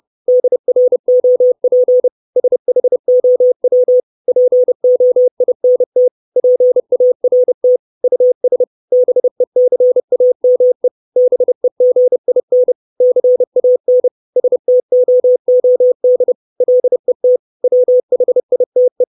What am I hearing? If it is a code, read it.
Morse code, heard as a keyed tone, about 30 wpm, 500 Hz. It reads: DROP SHOW POINT PART US BECAME BEGIN CAN STOOD LET WHITE
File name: 632 Words 30WPM